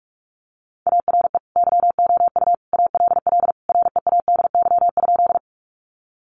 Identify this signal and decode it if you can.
Morse code, heard as a keyed tone, about 44 wpm, 700 Hz. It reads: ARE YOU ALL READY?
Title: Sentences from Top 400 Words 45wpm